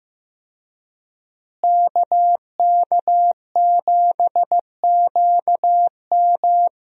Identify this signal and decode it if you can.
Morse code, heard as a keyed tone, about 16 wpm, 700 Hz. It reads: KK7QM
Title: USA Call Signs - 5 Characters - Repeated 3x - 15wpm